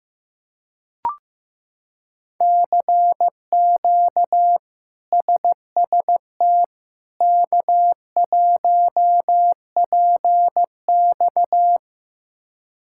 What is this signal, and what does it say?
Morse code, heard as a keyed tone, about 15 wpm, 700 Hz. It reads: CQ SST K1PX
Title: K1USN SST QSO Practice - 15wpm